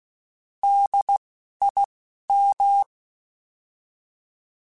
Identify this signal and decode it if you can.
Morse code, heard as a keyed tone, about 16 wpm, 800 Hz. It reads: D I M
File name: Alphabetical / Alphabetical mp3s / Ds @10